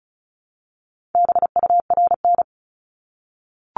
Morse code, heard as a keyed tone, about 36 wpm, 700 Hz. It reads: BURN
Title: Multi-letter Phonemes Repeated 3x 35wpm